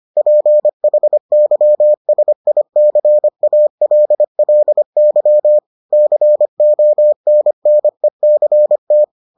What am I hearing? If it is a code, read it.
Morse code, heard as a keyed tone, about 25 wpm, 600 Hz. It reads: PHYSICALLY CONNECT